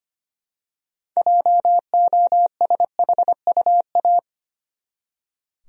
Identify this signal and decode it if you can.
Morse code, heard as a keyed tone, about 25 wpm, 700 Hz. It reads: JOSHUA
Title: USA Common Names - 25wpm